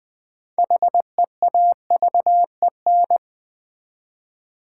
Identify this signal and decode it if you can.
Morse code, heard as a keyed tone, about 20 wpm, 700 Hz. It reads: HEAVEN
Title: 6 Letter Words - First 3 Letters Given - 20wpm